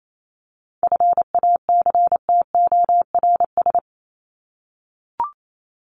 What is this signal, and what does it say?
Morse code, heard as a keyed tone, about 28 wpm, 700 Hz. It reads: FACTORS
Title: ES:S Suffix - Top 100 Words with Base Words 28wpm